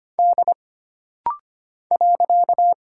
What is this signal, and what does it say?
Morse code, heard as a keyed tone, about 25 wpm, 700 Hz. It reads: D .
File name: Single Character (Letter:Number:Punctuation:Prosign) - Rapid-Fire - 25wpm